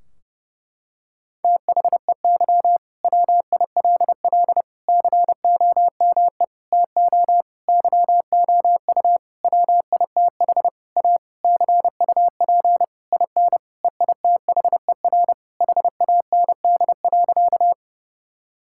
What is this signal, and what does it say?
Morse code, heard as a keyed tone, about 32 wpm, 700 Hz. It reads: THEY WILL COME TO YOU WITH A CUP IN EITHER HAND.
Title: Sentences from Top 1600 Words 30wpm